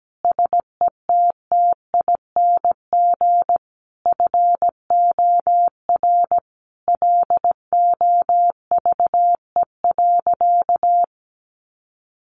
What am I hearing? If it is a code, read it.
Morse code, heard as a keyed tone, about 17 wpm, 700 Hz. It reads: SETTING FOR LOVE.
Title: Sentences from Top 1300 Words 17wpm